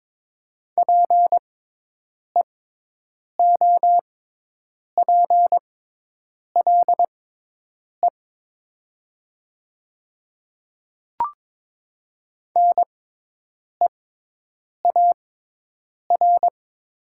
Morse code, heard as a keyed tone, about 22 wpm, spaced as for 8 wpm, 700 Hz. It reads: PEOPLE NEAR
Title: 6x Character Spacing - Top 300 Words - 22wpm